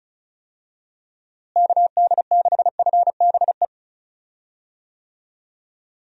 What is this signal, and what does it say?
Morse code, heard as a keyed tone, about 36 wpm, 700 Hz. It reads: KD6FBE
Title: USA Call Signs - 6 Characters - Repeated 2x - 35wpm